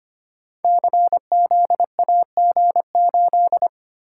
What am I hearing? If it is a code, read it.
Morse code, heard as a keyed tone, about 25 wpm, 700 Hz. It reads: CZAG8
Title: Code Groups - 5 Letter:Numbers - 25wpm